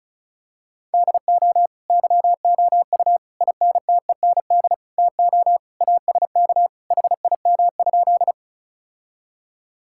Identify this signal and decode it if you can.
Morse code, heard as a keyed tone, about 36 wpm, 700 Hz. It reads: DO YOU INTEND TO ASK HIM?
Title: Sentences from Top 1600 Words 35wpm